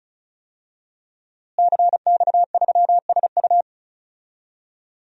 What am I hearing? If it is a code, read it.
Morse code, heard as a keyed tone, about 34 wpm, 700 Hz. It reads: CX3SU
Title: DXCC-100 Call Signs - 5 Characters - Repeated 3x - 35wpm